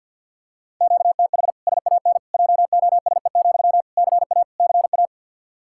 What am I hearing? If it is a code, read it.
Morse code, heard as a keyed tone, about 49 wpm, 700 Hz. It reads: QTH SAN JOSE, CA CA
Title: Practice QSO - Exchange 1 - Beginning - 50wpm